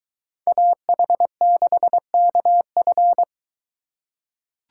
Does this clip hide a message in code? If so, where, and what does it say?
Yes: Morse code from 0.5 to 3.2 s, heard as a keyed tone, about 21 wpm, 700 Hz. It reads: AH6KF